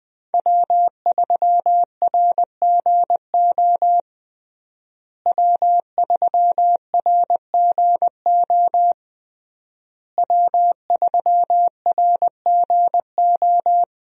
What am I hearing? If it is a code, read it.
Morse code, heard as a keyed tone, about 20 wpm, 700 Hz. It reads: W3RGO W3RGO W3RGO
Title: USA Call Signs - 5 Characters - Repeated 3x - 20wpm